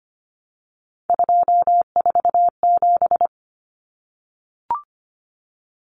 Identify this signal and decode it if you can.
Morse code, heard as a keyed tone, about 25 wpm, 700 Hz. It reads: 247
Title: Numbers - 3-Digit 25wpm